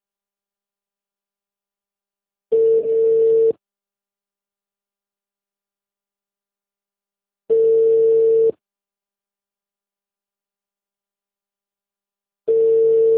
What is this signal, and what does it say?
Ringback tone (China, 450 Hz).